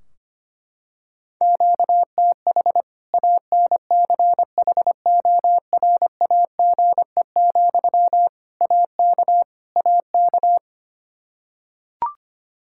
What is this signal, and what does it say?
Morse code, heard as a keyed tone, about 25 wpm, 700 Hz. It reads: QTH ANCHORAGE, AK AK
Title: K1USN SST QSO Practice 25wpm